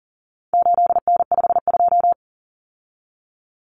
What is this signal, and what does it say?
Morse code, heard as a keyed tone, about 40 wpm, 700 Hz. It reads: 8N52
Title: Common Cut Numbers - 4-Digit 40wpm